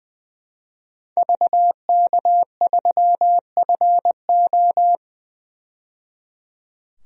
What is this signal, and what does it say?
Morse code, heard as a keyed tone, about 20 wpm, 700 Hz. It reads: VK3FO
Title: DXCC-100 Call Signs - 5 Characters - 20wpm